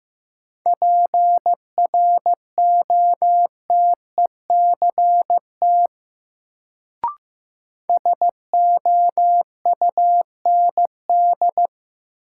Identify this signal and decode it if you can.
Morse code, heard as a keyed tone, about 15 wpm, 700 Hz. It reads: PROTECT SOUND
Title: Sentences from Top 900 Words - New Word Drill - 15wpm